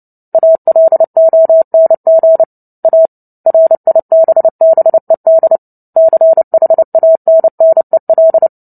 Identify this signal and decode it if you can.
Morse code, heard as a keyed tone, about 30 wpm, 650 Hz. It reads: ALONG A RIBBED CHANNEL
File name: Book Two / 03-Chapter 3 - 29 WPM